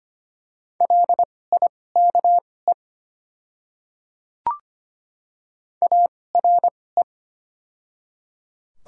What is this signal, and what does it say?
Morse code, heard as a keyed tone, about 25 wpm, spaced as for 18 wpm, 700 Hz. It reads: LIKE ARE
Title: 2x Character Spacing - Top 200 Words - 25wpm